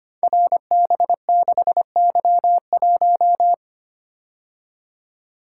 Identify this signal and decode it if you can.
Morse code, heard as a keyed tone, about 25 wpm, 700 Hz. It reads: RB6Y1
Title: Code Groups - 5 Letter:Numbers - 25wpm